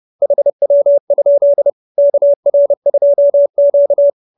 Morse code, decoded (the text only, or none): HW? KR2Q